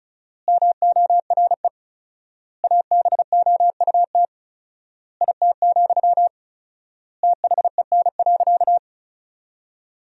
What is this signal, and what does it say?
Morse code, heard as a keyed tone, about 36 wpm, 700 Hz. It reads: MORE ABOUT IT, THEN.
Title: 4x Word-Spacing - Sentences from Top 200 Words - 35wpm